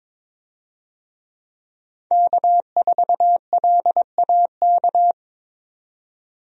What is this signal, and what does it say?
Morse code, heard as a keyed tone, about 21 wpm, 700 Hz. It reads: K4LAK
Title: USA Call Signs - 5 Characters - 22wpm